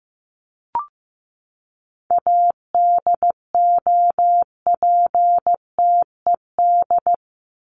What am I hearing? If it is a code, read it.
Morse code, heard as a keyed tone, about 15 wpm, 700 Hz. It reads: ADOPTED
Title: ED Suffix - Top 100 Words with Base Words 15wpm